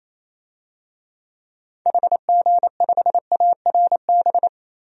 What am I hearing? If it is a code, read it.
Morse code, heard as a keyed tone, about 28 wpm, 700 Hz. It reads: HG5ARB
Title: DXCC-100 Call Signs - 6 Characters - Repeated 2x - 28wpm